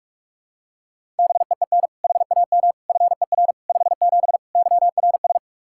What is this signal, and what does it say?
Morse code, heard as a keyed tone, about 41 wpm, 700 Hz. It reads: BEEN HAM FER 57 YRS
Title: K1USN SST QSO Practice 45wpm